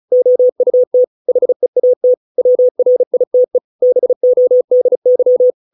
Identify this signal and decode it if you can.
Morse code, heard as a keyed tone, about 34 wpm, 500 Hz. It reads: OUT HEAT WRITE BODY